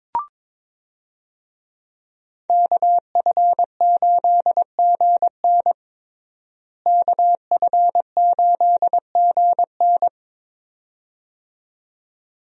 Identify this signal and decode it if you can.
Morse code, heard as a keyed tone, about 22 wpm, 700 Hz. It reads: KF8GN KF8GN
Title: USA Call Signs - 5 Characters - Repeated 2x - 22wpm